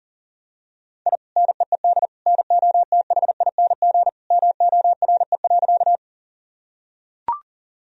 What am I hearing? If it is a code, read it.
Morse code, heard as a keyed tone, about 37 wpm, 700 Hz. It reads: I NEED NOTHING MORE.
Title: Sentences from Top 1000 Words - Review Part 1 - 40wpm